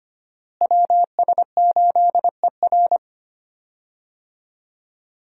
Morse code, heard as a keyed tone, about 25 wpm, 700 Hz. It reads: WS8ER
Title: USA Call Signs - 5 Characters - Repeated 3x - 25wpm